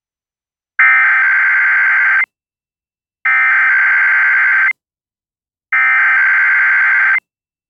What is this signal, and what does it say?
Emergency Alert System (SAME) tones (header bursts), heard as harsh buzzing bursts. It is heard from 0.8 to 7.1 s.